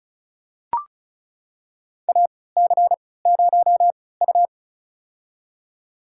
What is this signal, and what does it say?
Morse code, heard as a keyed tone, about 36 wpm, spaced as for 20 wpm, 700 Hz. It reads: AC0U
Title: USA Call Signs - 4 Characters - 3x Character Spacing - 35wpm